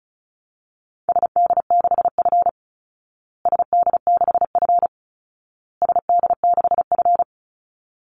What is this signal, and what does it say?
Morse code, heard as a keyed tone, about 34 wpm, 700 Hz. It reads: SD6F SD6F SD6F
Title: Call Signs Repeated 2x - 35wpm